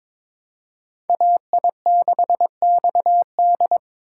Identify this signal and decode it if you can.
Morse code, heard as a keyed tone, about 22 wpm, 700 Hz. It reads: AI6XD
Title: USA Call Signs - 5 Characters - 2x Character Spacing - 22wpm